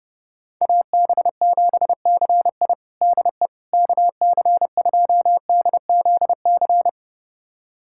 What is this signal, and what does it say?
Morse code, heard as a keyed tone, about 30 wpm, 700 Hz. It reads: AB7CI DE KC2DZC
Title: USA Callsign DE Callsign 30wpm